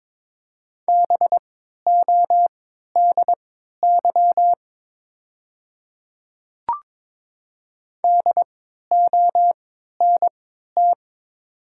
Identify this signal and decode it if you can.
Morse code, heard as a keyed tone, about 22 wpm, spaced as for 12 wpm, 690 Hz. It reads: BODY DONT
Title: 3x Character Spacing - Top 500 Words - 22wpm